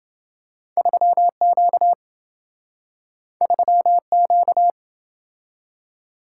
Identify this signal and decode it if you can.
Morse code, heard as a keyed tone, about 27 wpm, 700 Hz. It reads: 3Q 3Q